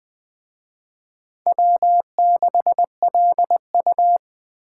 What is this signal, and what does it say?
Morse code, heard as a keyed tone, about 21 wpm, 700 Hz. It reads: W6LU